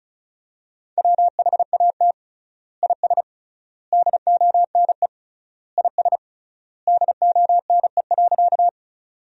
Morse code, heard as a keyed tone, about 34 wpm, 700 Hz. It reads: WHAT IS DONE IS DONE.
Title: 3x Word-Spacing - Sentences from Top 500 Words - 35wpm